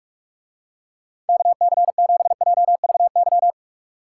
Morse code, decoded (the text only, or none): KC7JVY